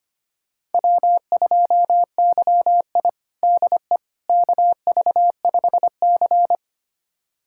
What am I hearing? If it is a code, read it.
Morse code, heard as a keyed tone, about 25 wpm, 700 Hz. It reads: W2YI DE KV5C
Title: USA Callsign DE Callsign - 4 Characters - 25wpm